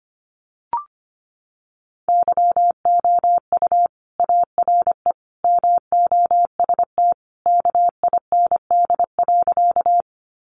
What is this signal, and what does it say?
Morse code, heard as a keyed tone, about 25 wpm, 700 Hz. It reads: YOU ARE MOST KIND.
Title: Sentences from Top 200 Words 25wpm